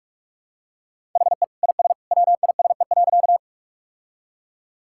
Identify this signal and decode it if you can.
Morse code, heard as a keyed tone, about 45 wpm, 700 Hz. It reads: HE IS WISE.
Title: Sentences from Top 1600 Words 45wpm